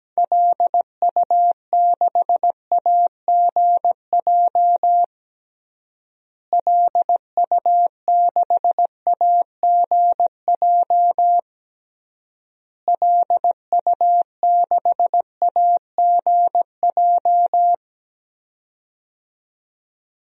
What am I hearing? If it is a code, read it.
Morse code, heard as a keyed tone, about 17 wpm, 700 Hz. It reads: LU6AGJ LU6AGJ LU6AGJ
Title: DXCC-100 Call Signs - 6 Characters - Repeated 3x - 17wpm